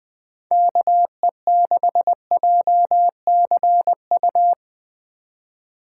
Morse code, heard as a keyed tone, about 20 wpm, 700 Hz. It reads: KE6JCU
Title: Speed Racing - USA 2x3 Call Signs 17wpm